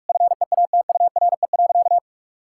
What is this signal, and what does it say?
Morse code, heard as a keyed tone, about 41 wpm, 700 Hz. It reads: FEATURE.
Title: Sentences from Top 1000 Words - Review Part 5 - 45wpm